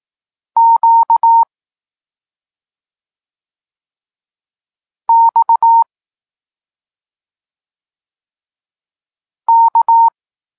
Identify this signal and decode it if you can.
Morse code, heard as a keyed tone, about 18 wpm, 920 Hz. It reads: Q X K
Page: Tekst 32 – Doskonalenie odbioru poznanych liter